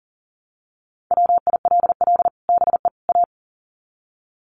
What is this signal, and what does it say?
Morse code, heard as a keyed tone, about 40 wpm, 700 Hz. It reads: WILL BE A
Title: Sets of 3 Words 40wpm